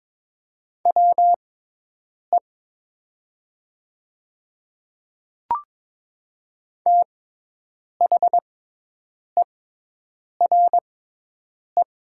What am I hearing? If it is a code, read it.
Morse code, heard as a keyed tone, about 22 wpm, spaced as for 8 wpm, 700 Hz. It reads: WE THERE
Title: 6x Character Spacing - Top 200 Words - 22wpm